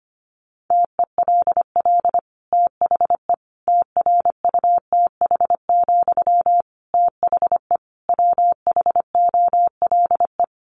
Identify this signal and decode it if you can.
Morse code, heard as a keyed tone, about 25 wpm, 700 Hz. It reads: TELL THE TRUTH, THE WHOLE
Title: Sentences from Top 800 Words 25wpm